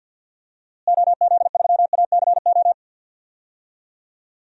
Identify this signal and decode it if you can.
Morse code, heard as a keyed tone, about 52 wpm, 700 Hz. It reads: OZ3ACO
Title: DXCC-100 Call Signs - 6 Characters - Repeated 3x - 50wpm